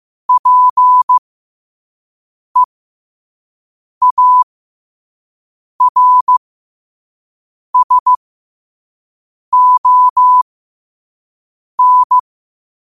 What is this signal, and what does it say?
Morse code, heard as a keyed tone, about 15 wpm, spaced as for 5 wpm, 1000 Hz. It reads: PEARSON
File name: qotdfiles / 5wpm